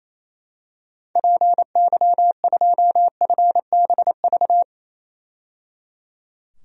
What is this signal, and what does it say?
Morse code, heard as a keyed tone, about 28 wpm, 700 Hz. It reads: PY2FBV
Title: DXCC-100 Call Signs - 6 Characters - 28wpm